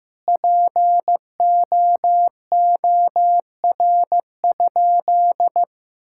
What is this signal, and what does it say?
Morse code, heard as a keyed tone, about 15 wpm, 700 Hz. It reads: POOR?
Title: Sentences from Top 1600 Words 15wpm